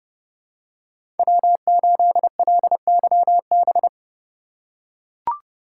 Morse code, heard as a keyed tone, about 29 wpm, 700 Hz. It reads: W8LYB
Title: USA Call Signs - 5 Characters - 2x Character Spacing - 30wpm